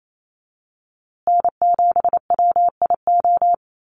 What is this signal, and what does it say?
Morse code, heard as a keyed tone, about 28 wpm, 700 Hz. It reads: N7WIO
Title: USA Call Signs Repeated 2x 28wpm